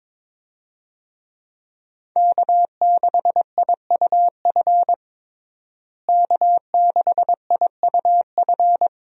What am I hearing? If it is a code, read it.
Morse code, heard as a keyed tone, about 22 wpm, 700 Hz. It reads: K6IUF K6IUF
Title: USA Call Signs - 5 Characters - Repeated 2x - 22wpm